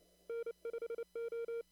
Morse code, heard as a keyed tone, about 29 wpm, 460 Hz. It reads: N5O
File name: CW / kt5j wpx cw-op-k5pi